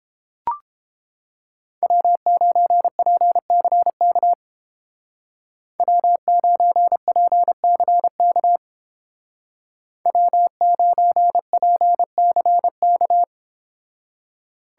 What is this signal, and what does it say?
Morse code, heard as a keyed tone, about 31 wpm, 700 Hz. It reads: W9PCK W9PCK W9PCK